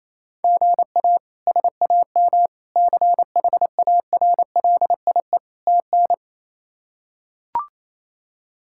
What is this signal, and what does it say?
Morse code, heard as a keyed tone, about 27 wpm, 700 Hz. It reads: GA SAM CHARLIE TN
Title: K1USN SST QSO Practice - 28wpm